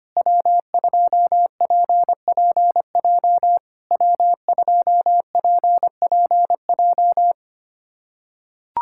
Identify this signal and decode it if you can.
Morse code, heard as a keyed tone, about 25 wpm, 700 Hz. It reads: W2PPJ W2PPJ
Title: SOTA QSO Practice 25wpm